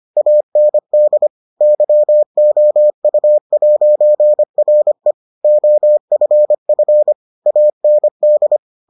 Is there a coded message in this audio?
Yes: Morse code from 0.2 to 8.6 s, heard as a keyed tone, about 25 wpm, 600 Hz. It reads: AND YOU'RE OFF AND